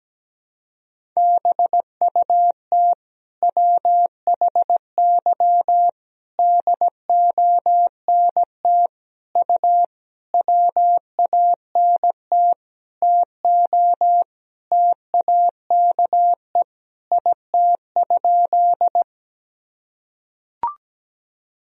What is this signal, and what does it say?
Morse code, heard as a keyed tone, about 17 wpm, 700 Hz. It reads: BUT WHY DONT U WANT TO TAKE IT?
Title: Sentences from Top 300 Words and Abbreviations 17wpm